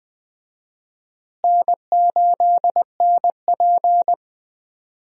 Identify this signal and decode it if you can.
Morse code, heard as a keyed tone, about 20 wpm, 700 Hz. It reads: N8NP
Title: USA Call Signs - 4 Characters - Repeated 2x - 20wpm